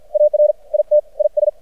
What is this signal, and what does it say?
Morse code, heard as a keyed tone, about 50 wpm, 600 Hz. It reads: AR IT IS
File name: archived / CRSnet